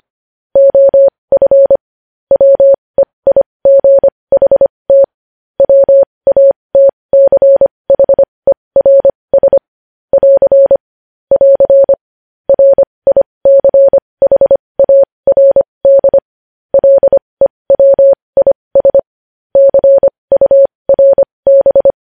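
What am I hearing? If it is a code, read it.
Morse code, heard as a keyed tone, about 25 wpm, spaced as for 20 wpm, 550 Hz. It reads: OF WEIGHT WATCHERS <AR> <AR> RICHARD LEWIS CURB
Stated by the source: News headlines in Morse at 20 wpm.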